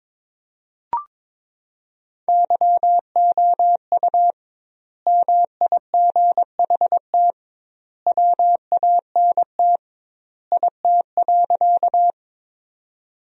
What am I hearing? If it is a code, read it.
Morse code, heard as a keyed tone, about 22 wpm, 700 Hz. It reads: YOU MIGHT WANT IT.
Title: 2x Word-Spacing - Sentences from Top 300 Words - 22wpm